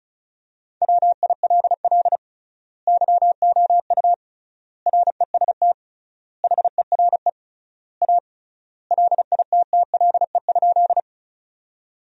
Morse code, heard as a keyed tone, about 36 wpm, 700 Hz. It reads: WILL YOU REST HERE A LITTLE?
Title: 3x Word-Spacing - Sentences from Top 500 Words - 35wpm